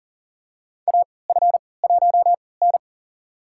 Morse code, heard as a keyed tone, about 40 wpm, spaced as for 23 wpm, 700 Hz. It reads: AF1N
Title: USA Call Signs - 3x Character Spacing - 40wpm